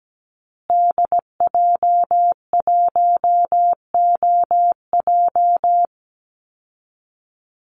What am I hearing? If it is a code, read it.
Morse code, heard as a keyed tone, about 17 wpm, 700 Hz. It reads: DJ1OJ
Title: Call Signs Repeated 2x - 17wpm